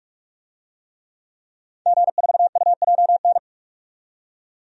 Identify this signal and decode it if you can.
Morse code, heard as a keyed tone, about 44 wpm, 690 Hz. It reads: G4UJN
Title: DXCC-100 Call Signs - 5 Characters - 45wpm